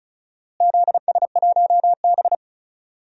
Morse code, heard as a keyed tone, about 36 wpm, 700 Hz. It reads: ZS1B
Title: DXCC-100 Call Signs - 4 Characters - Repeated 2x - 35wpm